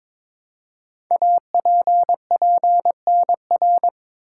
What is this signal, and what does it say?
Morse code, heard as a keyed tone, about 22 wpm, 700 Hz. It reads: APPNR
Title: Code Groups - 5 Letter:Numbers - 22wpm